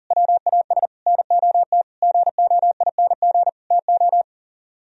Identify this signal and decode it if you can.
Morse code, heard as a keyed tone, about 52 wpm, 700 Hz. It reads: WAS NOT GOING TO